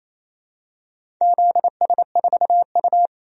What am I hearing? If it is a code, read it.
Morse code, heard as a keyed tone, about 28 wpm, 700 Hz. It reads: ZS4U
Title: DXCC-100 Call Signs - 4 Characters - 28wpm